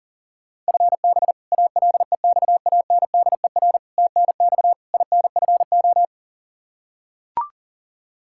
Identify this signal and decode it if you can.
Morse code, heard as a keyed tone, about 37 wpm, 700 Hz. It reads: FB ALEXANDER TNX INFO
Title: K1USN SST QSO Practice 40wpm